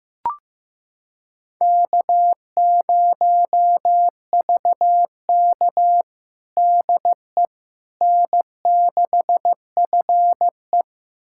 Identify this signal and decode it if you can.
Morse code, heard as a keyed tone, about 15 wpm, 700 Hz. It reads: K0VK DE N6FE
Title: USA Callsign DE Callsign - 4 Characters - 15wpm